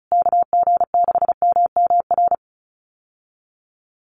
Morse code, heard as a keyed tone, about 34 wpm, 700 Hz. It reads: KG6MMR
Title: USA Call Signs Repeated 3x 35wpm